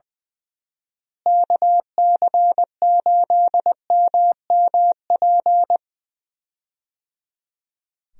Morse code, heard as a keyed tone, about 21 wpm, 700 Hz. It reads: KC8MMP
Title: USA Call Signs - 6 Characters - Repeated 3x - 20wpm